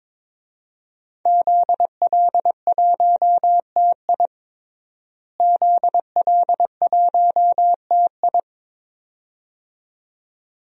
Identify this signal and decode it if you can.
Morse code, heard as a keyed tone, about 22 wpm, 700 Hz. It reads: ZL1TI ZL1TI
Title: DXCC-100 Call Signs - 5 Characters - Repeated 2x - 22wpm